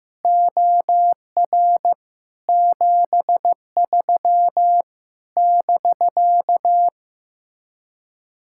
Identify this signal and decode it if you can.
Morse code, heard as a keyed tone, about 15 wpm, 700 Hz. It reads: OR 73 <BK>
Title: Speed-Racing Common QSO Phrases Encore - 15wpm